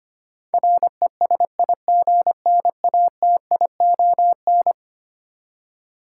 Morse code, heard as a keyed tone, about 25 wpm, 700 Hz. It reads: RESIGNATION
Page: morse-code-ninja-practice-25wpm